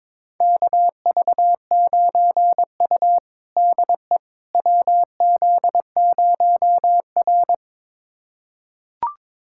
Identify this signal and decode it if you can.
Morse code, heard as a keyed tone, about 22 wpm, 700 Hz. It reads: KV9U DE WZ0R
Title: USA Callsign DE Callsign - 4 Characters - 22wpm